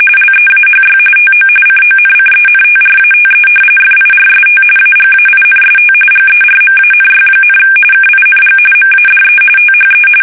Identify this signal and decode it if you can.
RTTY, about 45.45 baud. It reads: LEAHY TO CONCENTRATE FIRST AGAINST GERMANY, KING MADE THE
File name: fsk850